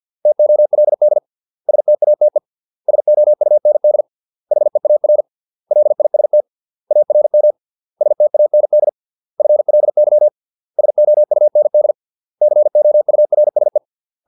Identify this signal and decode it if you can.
Morse code, heard as a keyed tone, about 50 wpm, 600 Hz. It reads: TOLD STATE SOUND HEAR LIST ARM STAND FLY SOUND COURSE